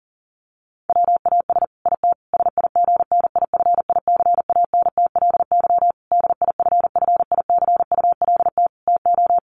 Morse code, heard as a keyed tone, about 37 wpm, 700 Hz. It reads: WAS IT SIGNIFICANTLY DIFFICULT TO
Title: Sentences from Top 1500 Words 40wpm